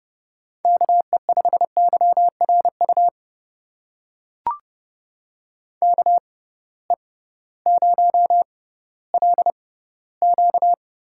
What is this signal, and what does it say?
Morse code, heard as a keyed tone, about 30 wpm, spaced as for 16 wpm, 700 Hz. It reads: KE5YRU KE0LQ
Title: USA Call Signs - 6x Character Spacing - 30wpm